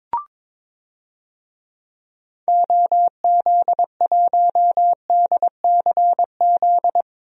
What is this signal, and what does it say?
Morse code, heard as a keyed tone, about 23 wpm, 700 Hz. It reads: OZ1DCZ
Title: DXCC-100 Call Signs - 6 Characters - Repeated 2x - 22wpm